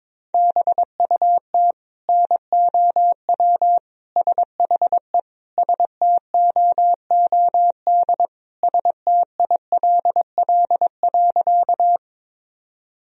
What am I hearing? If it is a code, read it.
Morse code, heard as a keyed tone, about 22 wpm, 700 Hz. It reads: BUT NOW SHE STOOD STILL.
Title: Sentences from Top 1000 Words - Review Part 2 - 22wpm